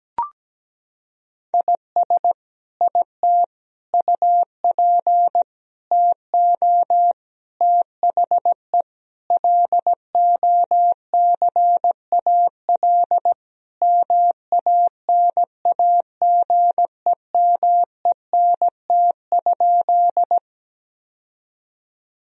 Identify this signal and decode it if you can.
Morse code, heard as a keyed tone, about 17 wpm, 700 Hz. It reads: IS IT UP TO THE LOCAL MANAGEMENT?
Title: Sentences from Top 1000 Words - Review Part 4 - 17wpm